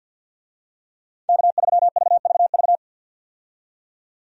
Morse code, heard as a keyed tone, about 52 wpm, 700 Hz. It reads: K3VVV
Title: USA Call Signs - 5 Characters - 50wpm